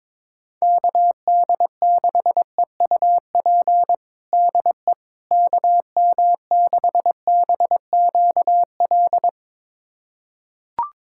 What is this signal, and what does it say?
Morse code, heard as a keyed tone, about 22 wpm, 700 Hz. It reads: KD6EUP DE KM6BQL
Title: USA Callsign DE Callsign - 6 Characters - 22wpm